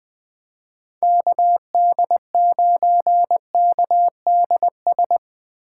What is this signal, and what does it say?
Morse code, heard as a keyed tone, about 20 wpm, 700 Hz. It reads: KD9KDS
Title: USA Call Signs - 6 Characters - 2x Character Spacing - 20wpm